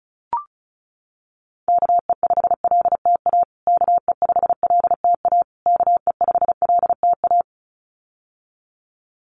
Morse code, heard as a keyed tone, about 34 wpm, 700 Hz. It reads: KE5LTA KE5LTA KE5LTA
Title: USA Call Signs Repeated 3x 35wpm